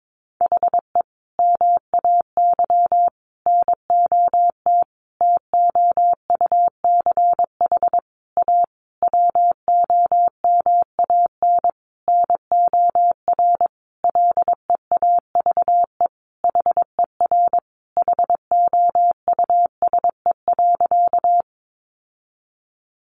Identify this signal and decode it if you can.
Morse code, heard as a keyed tone, about 22 wpm, 700 Hz. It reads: HE MAY NOT TOUCH A WOMAN NOR LEAVE HER HOUSE.
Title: Sentences from Top 1500 Words 22wpm